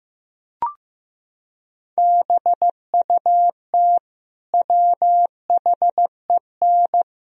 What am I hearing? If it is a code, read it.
Morse code, heard as a keyed tone, about 15 wpm, 700 Hz. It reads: BUT WHEN
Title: Sentences from Top 1000 Words - Review Part 3 - 15wpm